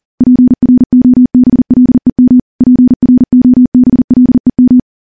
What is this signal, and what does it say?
Morse code, heard as a keyed tone, about 39 wpm, 250 Hz. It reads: PROBLEM PROBLEM
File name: morse / morse-training-40WPM